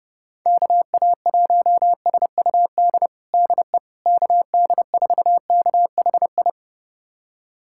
Morse code, heard as a keyed tone, about 30 wpm, 700 Hz. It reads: KA1SUD DE KD4KHI
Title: USA Callsign DE Callsign - 6 Characters - 30wpm